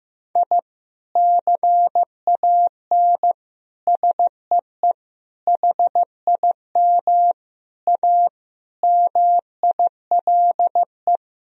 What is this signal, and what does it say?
Morse code, heard as a keyed tone, about 15 wpm, 700 Hz. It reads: I CAN SEE HIM A MILE
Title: Sentences from Top 1000 Words - Review Part 1 - 15wpm